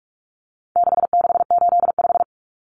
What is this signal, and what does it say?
Morse code, heard as a keyed tone, about 44 wpm, 700 Hz. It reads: 6685
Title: Common Cut Numbers - 4-Digit 45wpm